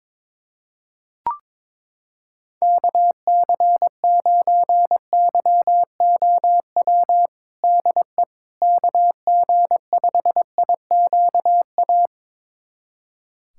Morse code, heard as a keyed tone, about 22 wpm, 700 Hz. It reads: KC9YOW DE KG5IQA
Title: USA Callsign DE Callsign - 6 Characters - 22wpm